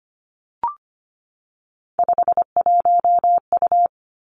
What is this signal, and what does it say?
Morse code, heard as a keyed tone, about 25 wpm, 700 Hz. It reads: 51U
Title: Three Challenging Letter-Number Combinations 25wpm